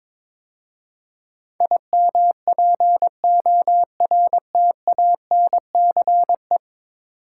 Morse code, heard as a keyed tone, about 22 wpm, 700 Hz. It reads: IMPORTANCE
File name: practice / 22wpm / 10 Letter Words - First 3 Letters Given - 22wpm